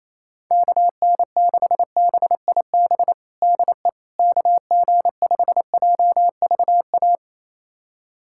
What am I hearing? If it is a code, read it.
Morse code, heard as a keyed tone, about 27 wpm, 700 Hz. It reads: KN6BIB DE KG5JVA
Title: USA Callsign DE Callsign - 6 Characters - 28wpm